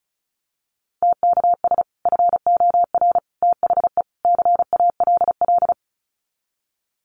Morse code, heard as a keyed tone, about 34 wpm, 700 Hz. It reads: TKS FOR THE CALL
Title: Common QSO Phrases 35wpm